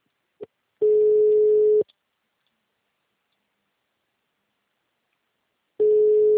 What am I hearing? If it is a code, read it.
Ringback tone (Europe, 425 Hz).